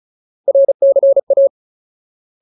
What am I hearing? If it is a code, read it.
Morse code, heard as a keyed tone, about 37 wpm, 550 Hz. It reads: RCA
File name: Ultimate Word List / 35WPM Ultimate